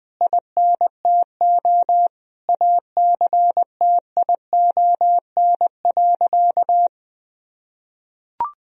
Morse code, heard as a keyed tone, about 21 wpm, 700 Hz. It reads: INTO ACTION.
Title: Sentences from Top 1000 Words - Review Part 4 - 20wpm